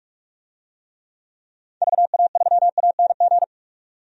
Morse code, heard as a keyed tone, about 42 wpm, 700 Hz. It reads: VA3ANG
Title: DXCC-100 Call Signs - 6 Characters - 45wpm